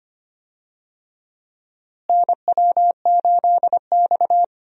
Morse code, heard as a keyed tone, about 25 wpm, 700 Hz. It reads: NW8X
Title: USA Call Signs - 4 Characters - Repeated 3x - 25wpm